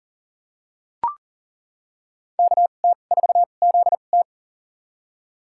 Morse code, heard as a keyed tone, about 40 wpm, spaced as for 29 wpm, 700 Hz. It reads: KT4ZT
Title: USA Call Signs - 5 Characters - 2x Character Spacing - 40wpm